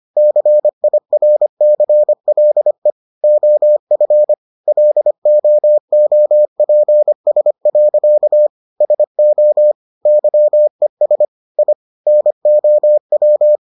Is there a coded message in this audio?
Yes: Morse code from 0.2 to 13.6 s, heard as a keyed tone, about 25 wpm, 600 Hz. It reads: CIRCLE OF LOOPS. SO YES I NOW